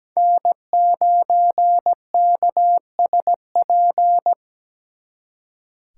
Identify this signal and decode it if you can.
Morse code, heard as a keyed tone, about 20 wpm, 700 Hz. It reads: N9KSP
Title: Speed Racing - USA 1x3 Call Signs 17wpm